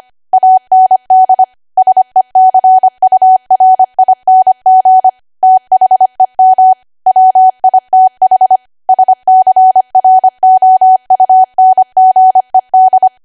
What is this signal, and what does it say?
Morse code, heard as a keyed tone, about 25 wpm, 750 Hz. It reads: AND SECURING THEM WITH SCROUNGED
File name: Archive / 25 WPM / 25WPM